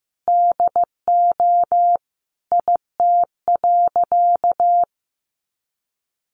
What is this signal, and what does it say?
Morse code, heard as a keyed tone, about 15 wpm, 700 Hz. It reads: DO IT.
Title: Sentences from Top 1400 Words 15wpm